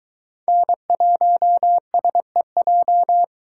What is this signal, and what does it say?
Morse code, heard as a keyed tone, about 22 wpm, 700 Hz. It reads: N1SEJ